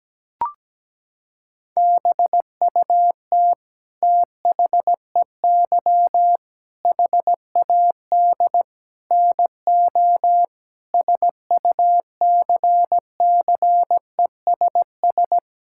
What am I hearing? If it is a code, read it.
Morse code, heard as a keyed tone, about 17 wpm, 700 Hz. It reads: BUT THEY HAD NO SUCCESS
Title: Sentences from Top 1000 Words - Review Part 6 - 17wpm